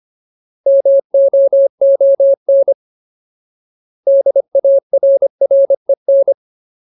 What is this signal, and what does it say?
Morse code, heard as a keyed tone, about 25 wpm, 550 Hz. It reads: MOON DARREN
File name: Ultimate Word List / 25WPM Ultimate